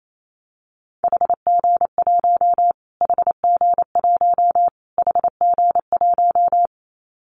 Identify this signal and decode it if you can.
Morse code, heard as a keyed tone, about 28 wpm, 700 Hz. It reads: HG1 HG1 HG1
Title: Callsign Prefixes for Top 25 DXCC - Repeated 3x - 28wpm